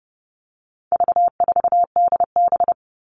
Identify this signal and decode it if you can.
Morse code, heard as a keyed tone, about 30 wpm, 700 Hz. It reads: V4DB
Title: All Cut Numbers - 4-Digit 30wpm